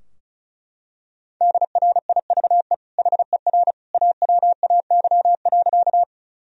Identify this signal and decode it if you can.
Morse code, heard as a keyed tone, about 36 wpm, 700 Hz. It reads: DRIVE HER AWAY.
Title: Sentences from Top 1000 Words - Review Part 2 - 35wpm